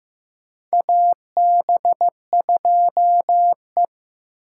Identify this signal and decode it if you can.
Morse code, heard as a keyed tone, about 16 wpm, 700 Hz. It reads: AB2E
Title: USA Call Signs - 4 Characters - 3x Character Spacing - 15wpm